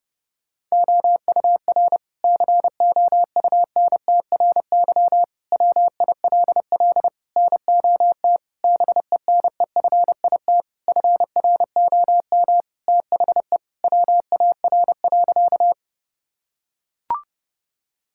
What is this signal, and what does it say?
Morse code, heard as a keyed tone, about 32 wpm, 700 Hz. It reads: OUR COUNTRY WILL NOT BENEFIT FROM THE WAR.
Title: Sentences from Top 1000 Words - Review Part 6 - 30wpm